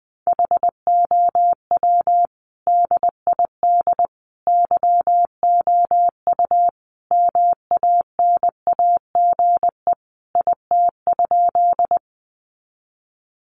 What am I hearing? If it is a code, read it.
Morse code, heard as a keyed tone, about 19 wpm, 700 Hz. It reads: HOW DID YOU MANAGE IT?
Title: Sentences from Top 1400 Words 20wpm